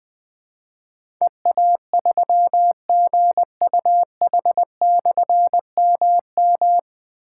Morse code, heard as a keyed tone, about 20 wpm, 700 Hz. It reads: EA3GUH/MM
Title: Difficult Call Signs 20wpm